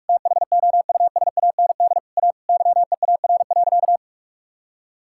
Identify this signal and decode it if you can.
Morse code, heard as a keyed tone, about 46 wpm, 700 Hz. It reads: THOUSAND A YEAR.
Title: Sentences from Top 1000 Words - Review Part 5 - 45wpm